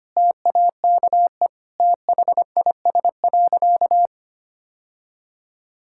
Morse code, heard as a keyed tone, about 25 wpm, 700 Hz. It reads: TAKE THIS.
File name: practice / 25wpm / Sentences with 2-Word Phrases - 25wpm